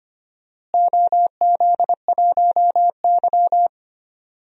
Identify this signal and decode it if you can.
Morse code, heard as a keyed tone, about 25 wpm, 700 Hz. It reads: OZ1Y
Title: DXCC-100 Call Signs - 4 Characters - Repeated 2x - 25wpm